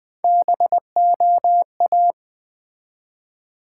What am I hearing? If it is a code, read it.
Morse code, heard as a keyed tone, about 20 wpm, 700 Hz. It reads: BOA
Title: Build A Word - 1 Letter At A Time - 3 to 5-Letter Words - 20wpm